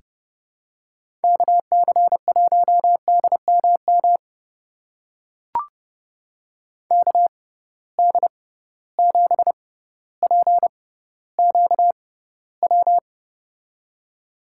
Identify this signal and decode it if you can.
Morse code, heard as a keyed tone, about 30 wpm, spaced as for 15 wpm, 700 Hz. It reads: KC1DMM KD7PQW
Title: USA Call Signs - 6 Characters - 6x Character Spacing - 30wpm